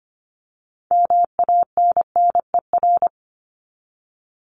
Morse code, heard as a keyed tone, about 25 wpm, 700 Hz. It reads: MANNER
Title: Top 1501-1750 Words 25wpm